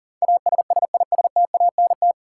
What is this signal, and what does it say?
Morse code, heard as a keyed tone, about 40 wpm, 700 Hz. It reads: ASSISTANT
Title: Practice QSO - Exchange 3 - More Info - 40wpm